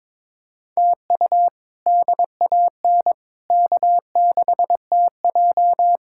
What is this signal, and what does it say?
Morse code, heard as a keyed tone, about 22 wpm, 700 Hz. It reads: TU DAN K6TJ
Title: K1USN SST QSO Practice - 22wpm